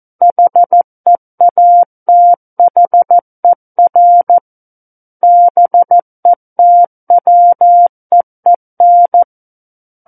Morse code, heard as a keyed tone, about 14 wpm, 700 Hz. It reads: HEATHER BETWEEN
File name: Book One / 05-Chapter 5 - 14 WPM